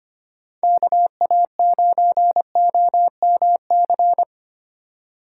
Morse code, heard as a keyed tone, about 31 wpm, 700 Hz. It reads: KA9OMC
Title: Speed Racing - USA Call Signs 25wpm